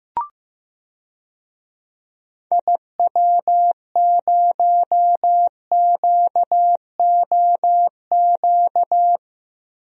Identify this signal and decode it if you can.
Morse code, heard as a keyed tone, about 15 wpm, 700 Hz. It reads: IW0QOQ
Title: DXCC-100 Call Signs - 6 Characters - 15wpm